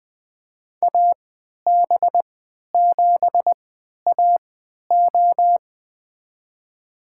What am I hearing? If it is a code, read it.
Morse code, heard as a keyed tone, about 20 wpm, spaced as for 11 wpm, 700 Hz. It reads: AB7AO
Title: USA Call Signs - 3x Character Spacing - 20wpm